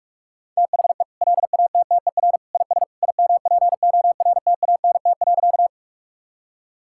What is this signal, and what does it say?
Morse code, heard as a keyed tone, about 46 wpm, 700 Hz. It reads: THE LATTER IS IMPORTANT.
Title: Sentences from Top 1600 Words 45wpm